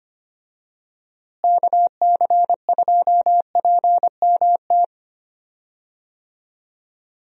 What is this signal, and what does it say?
Morse code, heard as a keyed tone, about 25 wpm, 700 Hz. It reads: KC2PMT
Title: USA Call Signs - 6 Characters - Repeated 2x - 25wpm